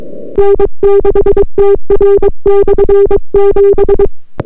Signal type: Morse code